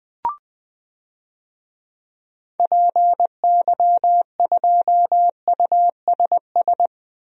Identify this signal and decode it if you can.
Morse code, heard as a keyed tone, about 20 wpm, 700 Hz. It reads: PY2USS
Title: DXCC-100 Call Signs - 6 Characters - 20wpm